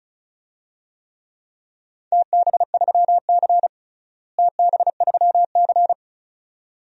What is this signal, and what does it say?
Morse code, heard as a keyed tone, about 36 wpm, 700 Hz. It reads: TB3C TB3C
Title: DXCC-100 Call Signs - 4 Characters - Repeated 2x - 35wpm